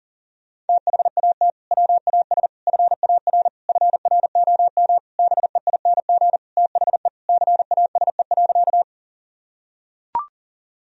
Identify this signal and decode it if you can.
Morse code, heard as a keyed tone, about 37 wpm, 700 Hz. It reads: THAT WAS FAR FROM BEING THE CASE.
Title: Sentences from Top 1000 Words - Review Part 3 - 40wpm